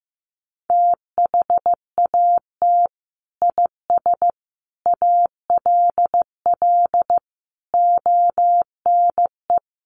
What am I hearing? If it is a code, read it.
Morse code, heard as a keyed tone, about 15 wpm, 700 Hz. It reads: THAT IS ALL ONE
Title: Sentences from Top 200 Words 15wpm